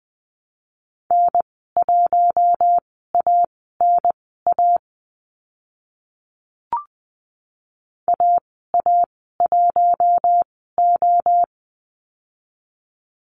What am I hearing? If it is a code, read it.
Morse code, heard as a keyed tone, about 20 wpm, spaced as for 14 wpm, 700 Hz. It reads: N1ANA AA1O
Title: 042 Lesson 8g - Number 1 - Call Signs 20wpm